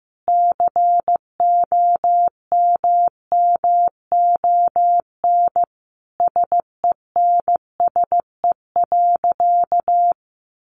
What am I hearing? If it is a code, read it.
Morse code, heard as a keyed tone, about 15 wpm, 700 Hz. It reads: COMMON SENSE.
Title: Sentences from Top 1300 Words 15wpm